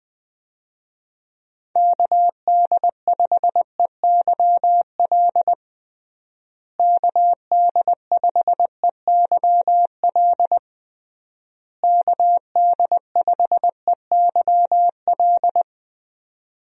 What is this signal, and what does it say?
Morse code, heard as a keyed tone, about 21 wpm, 700 Hz. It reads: KD5EYL KD5EYL KD5EYL
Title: USA Call Signs - 6 Characters - Repeated 3x - 20wpm